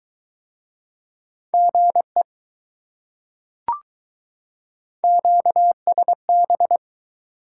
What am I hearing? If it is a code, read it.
Morse code, heard as a keyed tone, about 23 wpm, 700 Hz. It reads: GE QSB